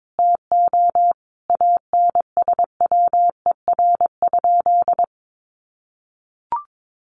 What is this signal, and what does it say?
Morse code, heard as a keyed tone, about 22 wpm, 700 Hz. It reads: TO ANSWER?
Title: Sentences from Top 1100 Words 22wpm